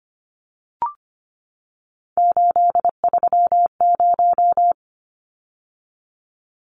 Morse code, heard as a keyed tone, about 25 wpm, 700 Hz. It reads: 830
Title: Numbers - 3-Digit 25wpm